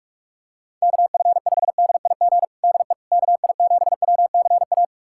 Morse code, heard as a keyed tone, about 48 wpm, 700 Hz. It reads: KF5DIG DE KI7WCA
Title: USA Callsign DE Callsign 45wpm